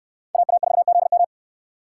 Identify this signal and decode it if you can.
Morse code, heard as a keyed tone, about 64 wpm, 700 Hz. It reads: AA4ZM